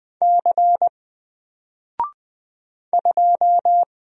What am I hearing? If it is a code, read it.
Morse code, heard as a keyed tone, about 20 wpm, 700 Hz. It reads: C 2
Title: Single Letter-Number Alt 20wpm